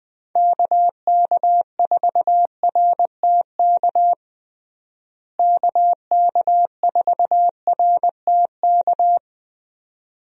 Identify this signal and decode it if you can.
Morse code, heard as a keyed tone, about 21 wpm, 700 Hz. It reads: KK4RTK KK4RTK
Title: USA Call Signs - 6 Characters - Repeated 2x - 20wpm